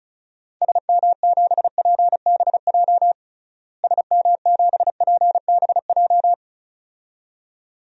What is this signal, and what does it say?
Morse code, heard as a keyed tone, about 36 wpm, 700 Hz. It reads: SM7PBJ SM7PBJ
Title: DXCC-100 Call Signs - 6 Characters - Repeated 2x - 35wpm